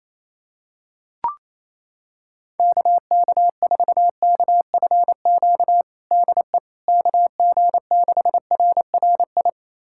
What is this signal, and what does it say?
Morse code, heard as a keyed tone, about 28 wpm, 700 Hz. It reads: KK4KFQ DE KG6RRI
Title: USA Callsign DE Callsign - 6 Characters - 28wpm